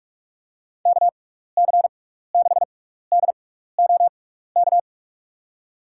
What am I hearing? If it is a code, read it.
Morse code, heard as a keyed tone, about 46 wpm, 700 Hz. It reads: K C 6 D O K
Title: USA Call Signs - 6 Characters - 6x Character Spacing - 45wpm